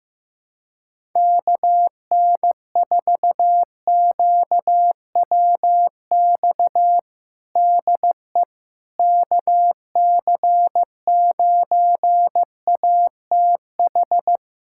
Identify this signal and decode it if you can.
Morse code, heard as a keyed tone, about 15 wpm, 700 Hz. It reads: KN4QWX DE KC9ATH
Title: USA Callsign DE Callsign - 6 Characters - 15wpm